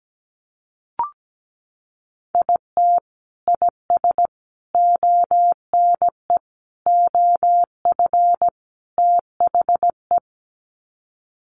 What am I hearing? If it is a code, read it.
Morse code, heard as a keyed tone, about 17 wpm, 700 Hz. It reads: IT IS ONE OF THE
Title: Sets of 5 Words 17wpm